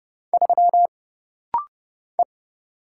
Morse code, heard as a keyed tone, about 30 wpm, 700 Hz. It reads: 3 E
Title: Single Letter-Number - Rapid-Fire 30wpm